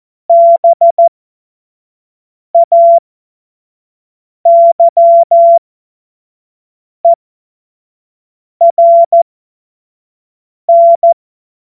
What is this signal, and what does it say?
Morse code, heard as a keyed tone, about 14 wpm, spaced as for 5 wpm, 680 Hz. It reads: BAYERN